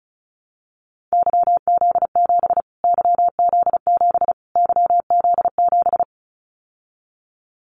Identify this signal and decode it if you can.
Morse code, heard as a keyed tone, about 34 wpm, 700 Hz. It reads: YZ7 YZ7 YZ7
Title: Callsign Prefixes for Top 25 DXCC - Repeated 3x - 35wpm